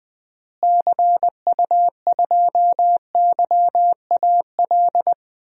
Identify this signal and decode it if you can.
Morse code, heard as a keyed tone, about 21 wpm, 700 Hz. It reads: CU2YAL
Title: DXCC-100 Call Signs - 6 Characters - Repeated 2x - 20wpm